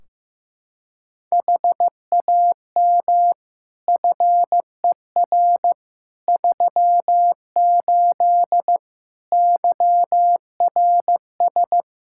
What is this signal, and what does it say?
Morse code, heard as a keyed tone, about 15 wpm, 700 Hz. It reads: HAM FER 38 YRS
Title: K1USN SST QSO Practice 15wpm